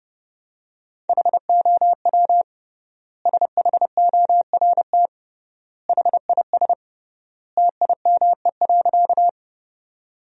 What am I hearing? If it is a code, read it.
Morse code, heard as a keyed tone, about 30 wpm, 700 Hz. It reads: HOW SHORT HIS TIME.
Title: 3x Word-Spacing - Sentences from Top 400 Words - 30wpm